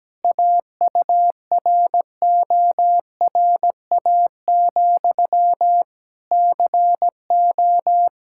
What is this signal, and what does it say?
Morse code, heard as a keyed tone, about 17 wpm, 700 Hz. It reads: AURORA, CO
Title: Practice QSO - Exchange 1 - Beginning - 17wpm